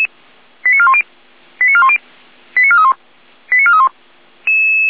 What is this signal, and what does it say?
Five-tone selective call (ZVEI), heard as quick runs of five tones, tones of about 70 ms. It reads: 89410 89410 89431 89431
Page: /455/SDR/ZVEI/89508/